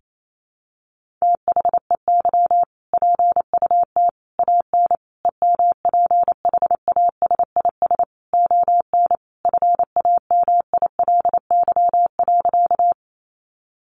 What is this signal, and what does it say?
Morse code, heard as a keyed tone, about 28 wpm, 700 Hz. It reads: THEY PUT AN EMPHASIS ON FAMILY.
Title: Sentences from Top 1400 Words 28wpm